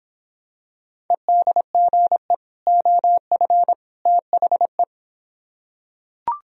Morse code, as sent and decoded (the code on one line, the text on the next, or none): . -.. --. / --- ..-. / - .... .
EDGE OF THE